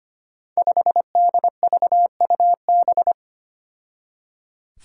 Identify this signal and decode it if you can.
Morse code, heard as a keyed tone, about 25 wpm, 700 Hz. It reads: 5DVUB
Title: Five Challenging Letter-Number Combinations 25wpm